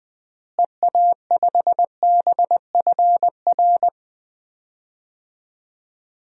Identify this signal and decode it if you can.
Morse code, heard as a keyed tone, about 20 wpm, 700 Hz. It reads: EA5BFR
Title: DXCC-100 Call Signs - 6 Characters - Repeated 3x - 20wpm